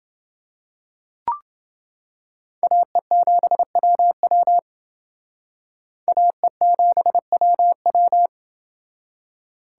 Morse code, heard as a keyed tone, about 28 wpm, 700 Hz. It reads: AE7WW AE7WW